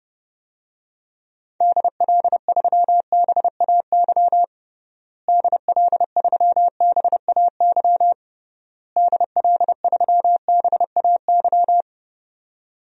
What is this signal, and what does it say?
Morse code, heard as a keyed tone, about 30 wpm, 700 Hz. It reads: DL3BAY DL3BAY DL3BAY
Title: DXCC-100 Call Signs - 6 Characters - Repeated 3x - 30wpm